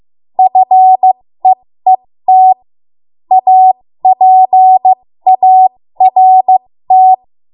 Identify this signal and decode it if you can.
Morse code, heard as a keyed tone, about 15 wpm, spaced as for 13 wpm, 750 Hz. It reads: FEET APART
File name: Morse / 13WPM